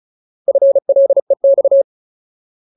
Morse code, heard as a keyed tone, about 34 wpm, 550 Hz. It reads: FLEX
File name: Ultimate Word List / 35WPM Ultimate